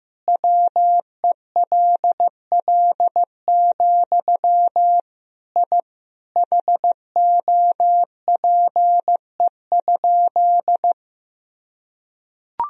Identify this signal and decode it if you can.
Morse code, heard as a keyed tone, about 15 wpm, 700 Hz. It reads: WELL, I HOPE?
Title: Sentences from Top 1000 Words - Review Part 3 - 15wpm